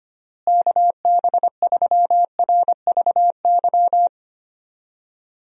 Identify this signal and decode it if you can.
Morse code, heard as a keyed tone, about 25 wpm, 700 Hz. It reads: KB3RVY
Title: USA Call Signs - 6 Characters - Repeated 3x - 25wpm